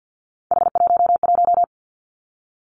Morse code, heard as a keyed tone, about 50 wpm, 700 Hz. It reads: H11
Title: Three Challenging Letter-Number Combinations 50wpm